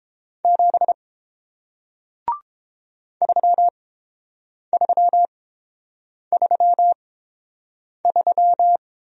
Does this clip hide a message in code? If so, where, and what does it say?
Yes: Morse code from 0.4 to 8.8 s, heard as a keyed tone, about 31 wpm, spaced as for 7 wpm, 700 Hz. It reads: 7 3333